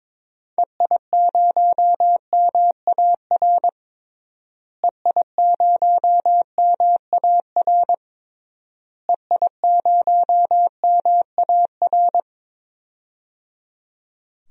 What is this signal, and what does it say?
Morse code, heard as a keyed tone, about 21 wpm, 700 Hz. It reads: EI0MAR EI0MAR EI0MAR
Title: DXCC-100 Call Signs - 6 Characters - Repeated 3x - 22wpm